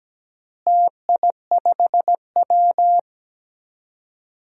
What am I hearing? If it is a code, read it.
Morse code, heard as a keyed tone, about 17 wpm, 700 Hz. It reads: TI5W
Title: DXCC-100 Call Signs - 4 Characters - Repeated 2x - 17wpm